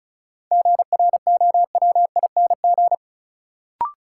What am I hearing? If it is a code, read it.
Morse code, heard as a keyed tone, about 36 wpm, 700 Hz. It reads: GROWING
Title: Sets of 5 Words Encore - New Word Drill - 35wpm